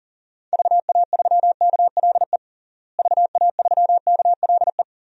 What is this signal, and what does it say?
Morse code, heard as a keyed tone, about 40 wpm, 700 Hz. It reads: VA3KLE VA3KLE
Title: DXCC-100 Call Signs - 6 Characters - Repeated 3x - 40wpm